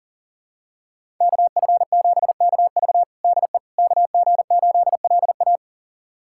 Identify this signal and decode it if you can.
Morse code, heard as a keyed tone, about 40 wpm, 700 Hz. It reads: KF7KV DE KG8LA
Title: USA Callsign DE Callsign - 5 Characters - 40wpm